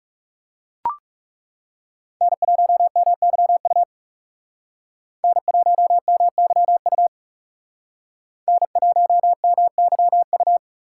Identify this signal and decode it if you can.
Morse code, heard as a keyed tone, about 36 wpm, 700 Hz. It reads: N1MYU N1MYU N1MYU
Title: Speed Racing - USA 1x3 Call Signs 30wpm